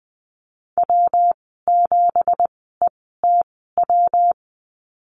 Morse code, heard as a keyed tone, about 20 wpm, spaced as for 14 wpm, 700 Hz. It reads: W7ETW
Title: 173 Lesson 28g - Number 9 - Call Signs 20wpm